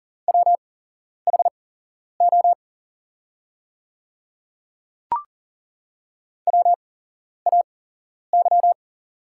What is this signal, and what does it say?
Morse code, heard as a keyed tone, about 40 wpm, spaced as for 11 wpm, 700 Hz. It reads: WHO WAY
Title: 8x Character Spacing - Top 100 Words - 40wpm